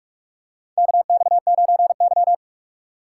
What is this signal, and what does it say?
Morse code, heard as a keyed tone, about 45 wpm, 700 Hz. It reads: KX9Y
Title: USA Call Signs - 4 Characters - 4x Character Spacing - 45wpm